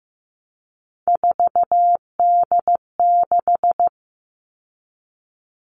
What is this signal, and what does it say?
Morse code, heard as a keyed tone, about 15 wpm, 700 Hz. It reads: 4D6
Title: All Cut Numbers - 3-Digit 15wpm